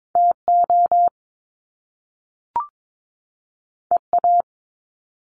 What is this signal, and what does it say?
Morse code, heard as a keyed tone, about 22 wpm, 700 Hz. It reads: TO EA
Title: Top Two Letter Combinations in Words 22wpm